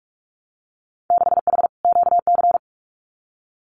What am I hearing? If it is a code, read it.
Morse code, heard as a keyed tone, about 46 wpm, 700 Hz. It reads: 6H QC
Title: Winter Field Day Exchanges Encore 45wpm